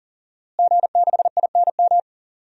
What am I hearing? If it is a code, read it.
Morse code, heard as a keyed tone, about 38 wpm, 690 Hz. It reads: G6INM
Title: DXCC-100 Call Signs - 5 Characters - Repeated 2x - 40wpm